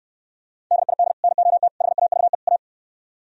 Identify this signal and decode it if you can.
Morse code, heard as a keyed tone, about 64 wpm, 700 Hz. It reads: DID NOT HAVE A